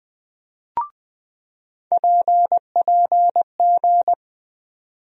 Morse code, heard as a keyed tone, about 20 wpm, 700 Hz. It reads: PPG
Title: USA Call Sign Postfixes - 3 Letters 20wpm